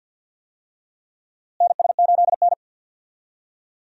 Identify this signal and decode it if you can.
Morse code, heard as a keyed tone, about 52 wpm, 700 Hz. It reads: NS8N
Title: USA Call Signs - 4 Characters - Repeated 3x - 50wpm